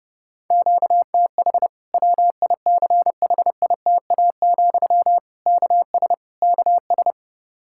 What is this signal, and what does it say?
Morse code, heard as a keyed tone, about 32 wpm, 700 Hz. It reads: QTH WICHITA, KS KS
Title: Practice QSO - Exchange 1 - Beginning - 30wpm